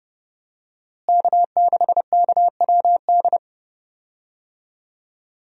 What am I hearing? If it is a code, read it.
Morse code, heard as a keyed tone, about 30 wpm, 700 Hz. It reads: K6KWD
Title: USA Call Signs - 5 Characters - Repeated 3x - 30wpm